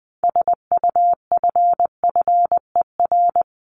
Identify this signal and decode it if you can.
Morse code, heard as a keyed tone, about 20 wpm, 700 Hz. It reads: SUFFER
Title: Top 1501-1750 Words 20wpm